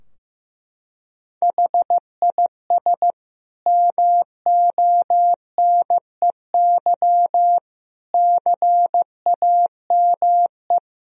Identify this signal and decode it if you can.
Morse code, heard as a keyed tone, about 15 wpm, 700 Hz. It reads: HIS MONEY CAME
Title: Sentences from Top 1000 Words - Review Part 4 - 15wpm